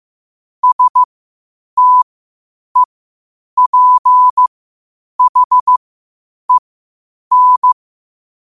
Morse code, heard as a keyed tone, about 15 wpm, spaced as for 8 wpm, 1000 Hz. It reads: STEPHEN
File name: qotdfiles / 8wpm